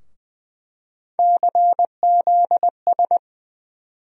Morse code, heard as a keyed tone, about 20 wpm, 700 Hz. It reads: CZS
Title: USA Call Sign Postfixes - 3 Letters 20wpm